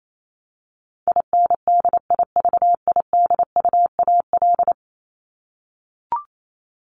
Morse code, heard as a keyed tone, about 28 wpm, 700 Hz. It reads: INDIVIDUAL
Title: ES:S Suffix - Top 100 Words with Base Words 28wpm